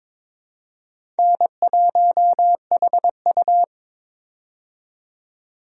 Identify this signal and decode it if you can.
Morse code, heard as a keyed tone, about 22 wpm, 700 Hz. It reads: N1HU
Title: USA Call Signs - 4 Characters - 22wpm